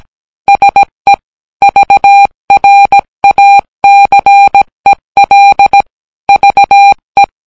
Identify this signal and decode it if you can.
Morse code, heard as a keyed tone, about 17 wpm, 800 Hz. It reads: SE VRACEL VE